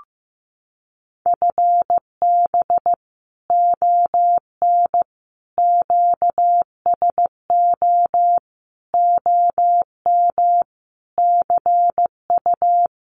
Morse code, heard as a keyed tone, about 15 wpm, 700 Hz. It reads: FB ON QSO OM CU
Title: Common QSO Phrases 15wpm